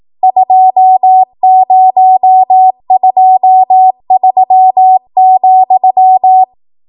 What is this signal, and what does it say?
Morse code, heard as a keyed tone, about 18 wpm, 750 Hz. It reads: 2023,